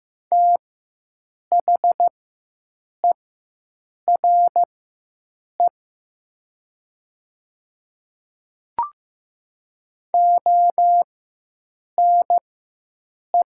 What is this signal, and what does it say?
Morse code, heard as a keyed tone, about 15 wpm, spaced as for 7 wpm, 700 Hz. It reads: THERE ONE
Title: 4x Character Spacing - Top 100 Words - 15wpm